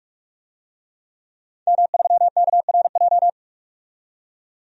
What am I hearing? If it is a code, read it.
Morse code, heard as a keyed tone, about 46 wpm, 700 Hz. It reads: M3KRJ
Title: DXCC-100 Call Signs - 5 Characters - 45wpm